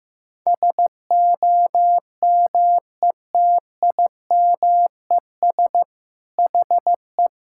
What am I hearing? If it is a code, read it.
Morse code, heard as a keyed tone, about 15 wpm, 700 Hz. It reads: SOMETIMES HE
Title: Sentences from Top 1000 Words - Review Part 5 - 15wpm